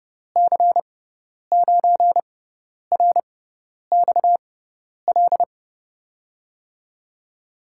Morse code, heard as a keyed tone, about 30 wpm, spaced as for 10 wpm, 700 Hz. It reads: C9RXL
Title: USA Call Signs - 6 Characters - 6x Character Spacing - 30wpm